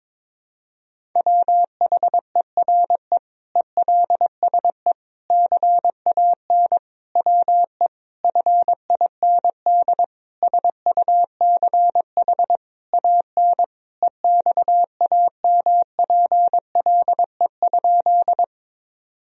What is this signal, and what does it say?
Morse code, heard as a keyed tone, about 22 wpm, 700 Hz. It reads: WHERE ELSE CAN WE FIND SUCH AN EXAMPLE?
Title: Sentences from Top 1000 Words - Review Part 1 - 22wpm